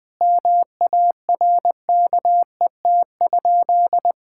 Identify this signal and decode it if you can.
Morse code, heard as a keyed tone, about 20 wpm, 700 Hz. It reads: MARKET?
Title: Sentences from Top 1000 Words - Review Part 6 - 20wpm